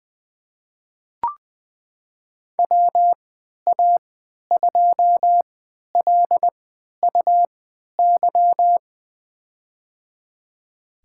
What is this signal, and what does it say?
Morse code, heard as a keyed tone, about 20 wpm, spaced as for 11 wpm, 700 Hz. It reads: WA2LUY
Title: USA Call Signs - 6 Characters - 3x Character Spacing - 20wpm